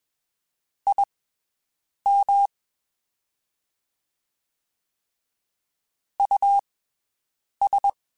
Morse code, heard as a keyed tone, about 21 wpm, spaced as for 7 wpm, 800 Hz. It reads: IM US
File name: (spaced) / QSOs Spaced-mp3s / WD0FBT 1 @20 sp